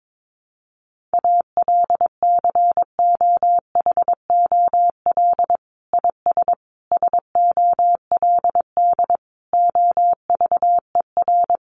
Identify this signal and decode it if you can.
Morse code, heard as a keyed tone, about 22 wpm, 700 Hz. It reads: ALCOHOL IS SOLD OVER
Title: Sentences from Top 1400 Words 22wpm